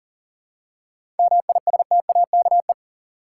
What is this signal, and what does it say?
Morse code, heard as a keyed tone, about 40 wpm, 700 Hz. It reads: MISTAKE
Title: Seven Letter Words - 40wpm